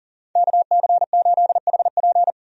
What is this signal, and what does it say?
Morse code, heard as a keyed tone, about 37 wpm, 700 Hz. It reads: KC8HP
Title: USA Call Signs - 5 Characters - 40wpm